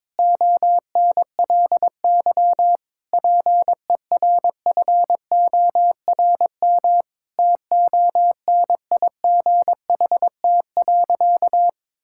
Morse code, heard as a keyed tone, about 22 wpm, 700 Hz. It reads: ONLY PERFORM TONIGHT.
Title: Sentences from Top 1000 Words - Review Part 5 - 22wpm